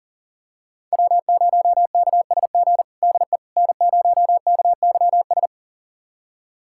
Morse code, heard as a keyed tone, about 40 wpm, 700 Hz. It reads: W0KSG DE N0KYS
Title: USA Callsign DE Callsign - 5 Characters - 40wpm